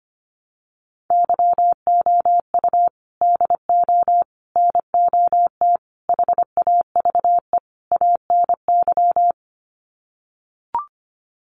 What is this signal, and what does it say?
Morse code, heard as a keyed tone, about 25 wpm, 700 Hz. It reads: YOU DO NOT HAVE ANY
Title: Sets of 5 Words Encore 25wpm